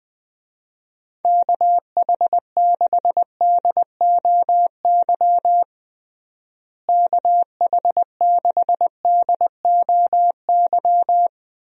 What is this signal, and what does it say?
Morse code, heard as a keyed tone, about 20 wpm, 700 Hz. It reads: KH6DOY KH6DOY
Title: DXCC-100 Call Signs - 6 Characters - Repeated 2x - 20wpm